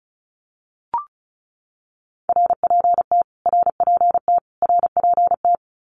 Morse code, heard as a keyed tone, about 34 wpm, 700 Hz. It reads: RPT RPT RPT
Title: Abbreviations Repeated 3x - 35wpm